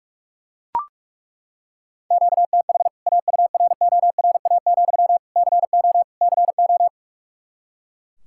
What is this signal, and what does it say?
Morse code, heard as a keyed tone, about 42 wpm, 700 Hz. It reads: QTH AURORA, CO CO
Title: K1USN SST QSO Practice 45wpm